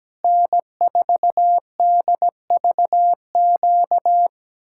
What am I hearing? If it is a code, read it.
Morse code, heard as a keyed tone, about 17 wpm, 700 Hz. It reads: N4DVQ
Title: Speed-Racing Common QSO Phrases - 17wpm